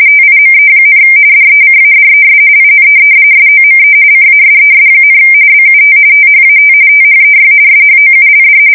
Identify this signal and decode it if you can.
RTTY, about 45.45 baud. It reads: TIME, DECADES IN SOME CASES, BUT THAT THEY SHOULD
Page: 170 Shift 2125/2295